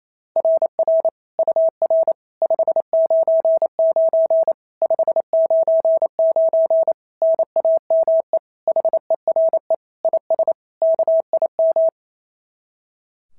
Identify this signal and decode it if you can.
Morse code, heard as a keyed tone, about 28 wpm, 690 Hz. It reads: RR UR 599 599 NAME HERE IS KIM
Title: SOTA QSO Practice 28wpm